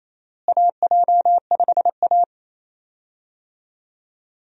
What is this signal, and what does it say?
Morse code, heard as a keyed tone, about 27 wpm, 700 Hz. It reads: AJ5A
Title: USA Call Signs - 4 Characters - Repeated 2x - 28wpm